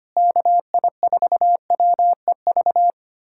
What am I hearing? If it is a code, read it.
Morse code, heard as a keyed tone, about 25 wpm, 700 Hz. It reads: KI4WEV
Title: USA Callsign DE Callsign - 6 Characters - 25wpm